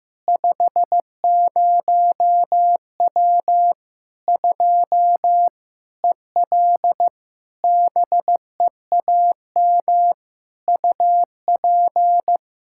Morse code, heard as a keyed tone, about 15 wpm, 700 Hz. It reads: 50W 2 EL BEAM UP
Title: Practice QSO - Exchange 2 - Info - 15wpm